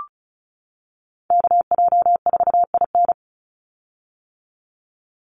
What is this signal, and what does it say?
Morse code, heard as a keyed tone, about 34 wpm, 700 Hz. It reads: KJ4IN
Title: USA Call Signs 35wpm